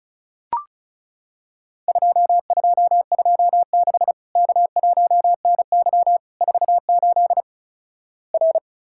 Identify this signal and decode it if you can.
Morse code, heard as a keyed tone, about 36 wpm, 700 Hz. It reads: 2226 K1NY 48 R
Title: Worked All Europe DX Contest - QTC Exchange - 35wpm